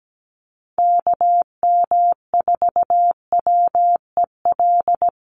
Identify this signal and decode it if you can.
Morse code, heard as a keyed tone, about 17 wpm, 700 Hz. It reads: KM4WEL
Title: USA Call Signs Repeated 2x 17wpm